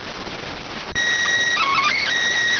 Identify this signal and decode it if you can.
SSTV (slow-scan television), PD 290.